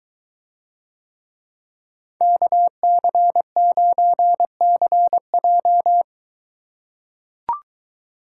Morse code, heard as a keyed tone, about 22 wpm, 700 Hz. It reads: KC9CJ